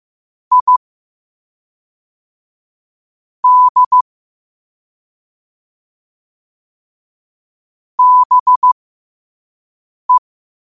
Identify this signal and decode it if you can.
Morse code, heard as a keyed tone, about 15 wpm, spaced as for 5 wpm, 1000 Hz. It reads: I D BE
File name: qotdfiles / 5wpm